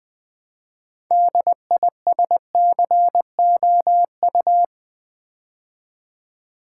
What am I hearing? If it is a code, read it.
Morse code, heard as a keyed tone, about 20 wpm, 700 Hz. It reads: DISCOU
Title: Build A Word - 3 Letters At A Time - 8-Letter Words - 20wpm